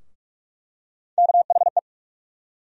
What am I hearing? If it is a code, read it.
Morse code, heard as a keyed tone, about 46 wpm, 700 Hz. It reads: KHE
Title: USA Call Sign Postfixes - 3 Letters 45wpm